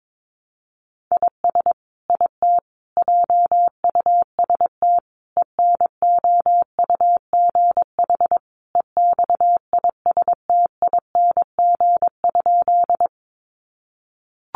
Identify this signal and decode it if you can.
Morse code, heard as a keyed tone, about 22 wpm, 700 Hz. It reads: IS IT JUST ENOUGH EXISTING?
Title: Sentences from Top 1500 Words 22wpm